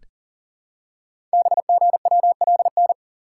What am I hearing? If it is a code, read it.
Morse code, heard as a keyed tone, about 38 wpm, 690 Hz. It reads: BGWLN
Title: Code Groups - 5 Letters - 40wpm